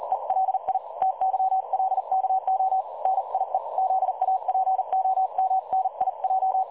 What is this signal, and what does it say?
Morse code, heard as a keyed tone, about 39 wpm, 760 Hz. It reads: ELI TYLKO NIE ZAPOMNIJ
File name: dlazaawansowanych / nagraniazpasma / websdr recording 3565.0kHz